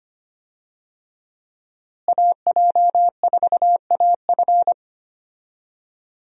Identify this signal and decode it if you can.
Morse code, heard as a keyed tone, about 25 wpm, 700 Hz. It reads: AJ4AF
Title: USA Call Signs - 5 Characters - Repeated 3x - 25wpm